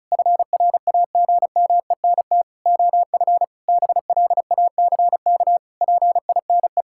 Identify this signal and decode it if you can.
Morse code, heard as a keyed tone, about 36 wpm, 700 Hz. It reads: FRAGMENT OF BLACK PINE
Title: Pangram Challenge v2 - 35wpm